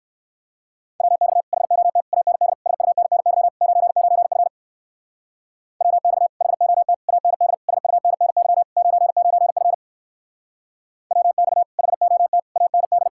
Morse code, heard as a keyed tone, about 60 wpm, 700 Hz. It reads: WX HOT AND SUNNY 99F WX HOT AND SUNNY 99F WX HOT AND